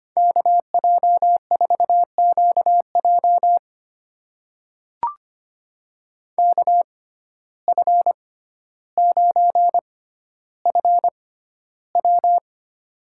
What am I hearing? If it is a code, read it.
Morse code, heard as a keyed tone, about 25 wpm, spaced as for 13 wpm, 700 Hz. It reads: KJ4QJ KF9FW
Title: USA Call Signs - 5 Characters - 6x Character Spacing - 25wpm